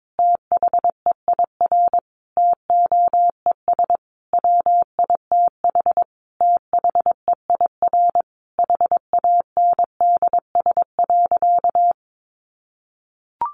Morse code, heard as a keyed tone, about 22 wpm, 700 Hz. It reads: THEIR TOES WITH THEIR HANDS.
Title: Easy Phrases 22wpm